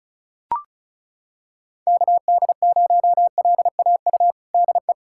Morse code, heard as a keyed tone, about 34 wpm, 700 Hz. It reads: KD0LAU DE
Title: USA Callsign DE Callsign 35wpm